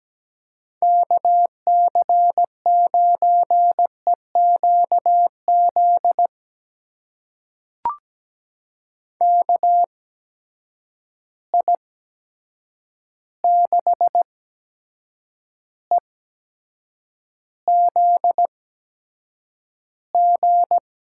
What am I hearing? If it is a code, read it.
Morse code, heard as a keyed tone, about 17 wpm, spaced as for 7 wpm, 700 Hz. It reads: KC9EQZ KI6EZG
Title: USA Call Signs - 6 Characters - 8x Character Spacing - 17wpm